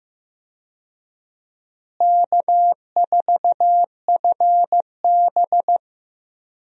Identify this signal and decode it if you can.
Morse code, heard as a keyed tone, about 15 wpm, 700 Hz. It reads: K4FB
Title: USA Call Signs - 4 Characters - Repeated 3x - 15wpm